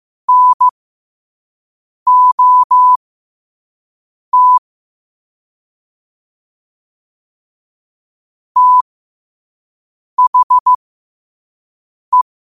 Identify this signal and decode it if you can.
Morse code, heard as a keyed tone, about 15 wpm, spaced as for 5 wpm, 1000 Hz. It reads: NOT THE
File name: qotdfiles / 5wpm